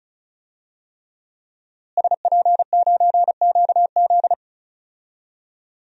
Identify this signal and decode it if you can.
Morse code, heard as a keyed tone, about 36 wpm, 700 Hz. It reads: SP9QZ
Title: DXCC-100 Call Signs - 5 Characters - Repeated 3x - 35wpm